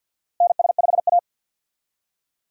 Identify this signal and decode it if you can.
Morse code, heard as a keyed tone, about 48 wpm, 700 Hz. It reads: NS5A
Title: USA Call Signs - 4 Characters - Repeated 3x - 50wpm